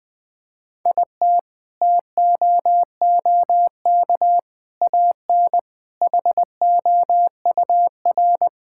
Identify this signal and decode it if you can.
Morse code, heard as a keyed tone, about 20 wpm, 700 Hz. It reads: IT TOOK AN HOUR
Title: Sentences with 5-Word Phrases - 20wpm